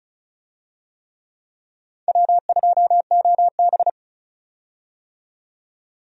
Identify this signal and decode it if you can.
Morse code, heard as a keyed tone, about 36 wpm, 690 Hz. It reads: W2OB
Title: USA Call Signs - 4 Characters - 35wpm